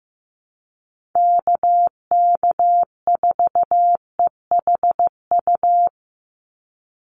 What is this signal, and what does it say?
Morse code, heard as a keyed tone, about 15 wpm, 700 Hz. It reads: KK4EHU
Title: USA Call Signs 15wpm